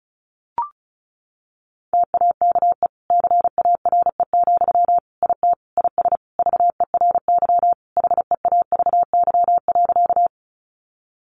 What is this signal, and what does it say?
Morse code, heard as a keyed tone, about 34 wpm, 700 Hz. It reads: TAKE CARE, IT IS VERY HEAVY.
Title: Sentences from Top 1100 Words 35wpm